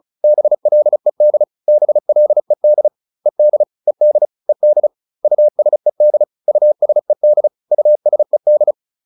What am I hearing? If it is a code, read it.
Morse code, heard as a keyed tone, about 36 wpm, 600 Hz. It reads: BLED BLED ED ED ED USED USED USED